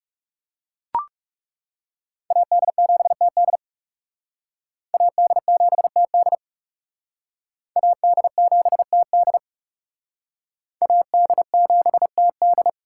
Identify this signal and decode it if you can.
Morse code, heard as a keyed tone, about 37 wpm, 700 Hz. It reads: AD7TD AD7TD AD7TD AD7TD
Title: Speed Racing - USA 2x2 Call Signs 30wpm